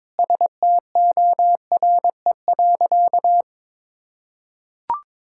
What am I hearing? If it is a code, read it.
Morse code, heard as a keyed tone, about 22 wpm, 700 Hz. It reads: STORE.
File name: practice / 22wpm / Sentences with 4-Word Phrases - 22wpm